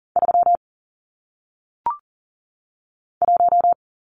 Morse code, heard as a keyed tone, about 40 wpm, 700 Hz. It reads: 3 1
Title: Numbers - 1-Digit 40wpm